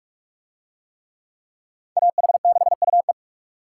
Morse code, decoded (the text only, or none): AH6UE